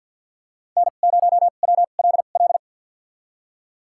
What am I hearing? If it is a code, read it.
Morse code, heard as a keyed tone, about 50 wpm, spaced as for 36 wpm, 700 Hz. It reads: N0WLL
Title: USA Call Signs - 5 Characters - 2x Character Spacing - 50wpm